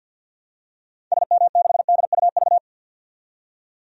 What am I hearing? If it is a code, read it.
Morse code, heard as a keyed tone, about 53 wpm, 700 Hz. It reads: SM6DUV
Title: DXCC-100 Call Signs - 6 Characters - 50wpm